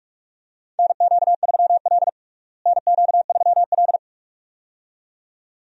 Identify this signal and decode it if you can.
Morse code, heard as a keyed tone, about 46 wpm, 700 Hz. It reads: NQ3L NQ3L
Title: USA Call Signs - 4 Characters - Repeated 2x - 45wpm